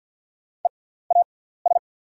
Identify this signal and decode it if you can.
Morse code, heard as a keyed tone, about 48 wpm, spaced as for 17 wpm, 700 Hz. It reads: EAS
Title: 6x Character Spacing - Top 500 Words - 50wpm